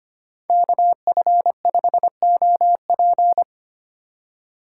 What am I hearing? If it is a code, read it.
Morse code, heard as a keyed tone, about 31 wpm, 700 Hz. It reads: KF5OP
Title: Speed Racing - USA 2x2 Call Signs 25wpm